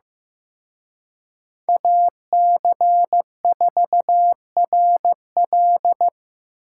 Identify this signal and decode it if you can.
Morse code, heard as a keyed tone, about 17 wpm, 700 Hz. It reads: AC4RL
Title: Speed Racing - USA 2x2 Call Signs 15wpm